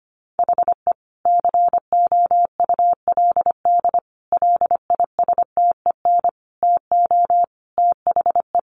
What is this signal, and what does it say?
Morse code, heard as a keyed tone, about 25 wpm, 700 Hz. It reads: HE COULD LISTEN TO THE
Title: Sentences from Top 1500 Words 25wpm